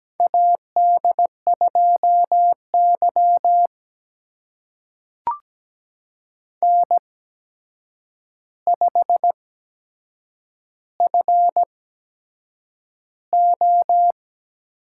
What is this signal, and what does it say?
Morse code, heard as a keyed tone, about 17 wpm, spaced as for 7 wpm, 700 Hz. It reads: AD2Y N5FO
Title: USA Call Signs - 4 Characters - 8x Character Spacing - 17wpm